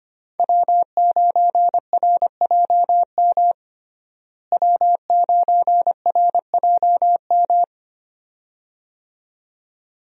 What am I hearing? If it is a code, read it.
Morse code, heard as a keyed tone, about 25 wpm, 700 Hz. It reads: W9RJM W9RJM
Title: USA Call Signs - 5 Characters - Repeated 2x - 25wpm